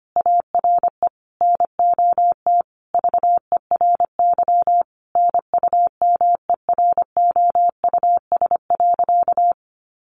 Morse code, heard as a keyed tone, about 25 wpm, 700 Hz. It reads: ARE NOT VERY NUMEROUS.
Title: Sentences from Top 1400 Words 25wpm